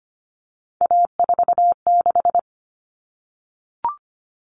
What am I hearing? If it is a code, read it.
Morse code, heard as a keyed tone, about 25 wpm, 700 Hz. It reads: A46
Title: Common Cut Numbers - 3-Digit 25wpm